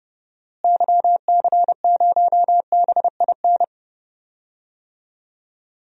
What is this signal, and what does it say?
Morse code, heard as a keyed tone, about 30 wpm, 700 Hz. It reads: YC0BIN
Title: DXCC-100 Call Signs - 6 Characters - Repeated 3x - 30wpm